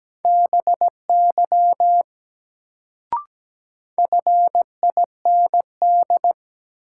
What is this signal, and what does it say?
Morse code, heard as a keyed tone, about 17 wpm, 700 Hz. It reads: BY FIND
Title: Sentences from Top 100 Words - New Word Drill - 17wpm